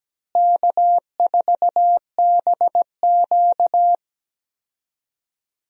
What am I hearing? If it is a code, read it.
Morse code, heard as a keyed tone, about 17 wpm, 700 Hz. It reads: K4BQ
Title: USA Call Signs - 4 Characters - Repeated 2x - 17wpm